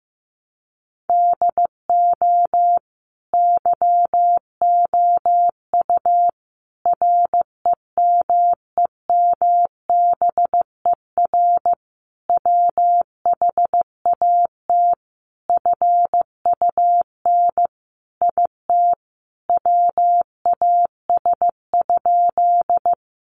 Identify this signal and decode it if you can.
Morse code, heard as a keyed tone, about 15 wpm, 700 Hz. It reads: DO YOU REMEMBER WHAT FUN IT WAS?
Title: Sentences from Top 700 Words 15wpm